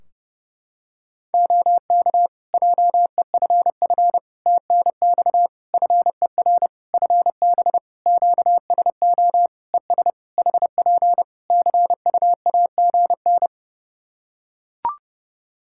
Morse code, heard as a keyed tone, about 32 wpm, 700 Hz. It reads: OK JEFF TNX FER FB QSO ES HP CUAGN
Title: Practice QSO - Exchange 4 - Ending - 30wpm